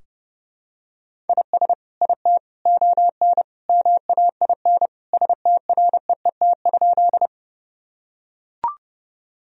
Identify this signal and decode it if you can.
Morse code, heard as a keyed tone, about 30 wpm, 700 Hz. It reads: IS IT ON MAIN STREET?
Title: Sentences from Top 1000 Words - Review Part 2 - 30wpm